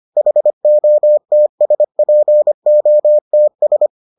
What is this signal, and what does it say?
Morse code, heard as a keyed tone, about 25 wpm, 600 Hz. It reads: HOTSPOTS